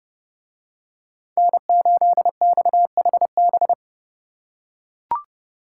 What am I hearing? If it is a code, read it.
Morse code, heard as a keyed tone, about 29 wpm, 700 Hz. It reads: N8XHB